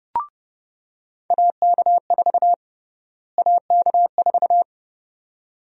Morse code, heard as a keyed tone, about 32 wpm, 700 Hz. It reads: AK4 AK4
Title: USA Call Sign Prefixes - Repeated 2x - 30wpm